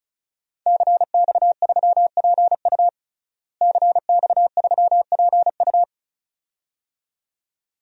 Morse code, heard as a keyed tone, about 34 wpm, 700 Hz. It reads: CX3PU CX3PU
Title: DXCC-100 Call Signs - 5 Characters - Repeated 2x - 35wpm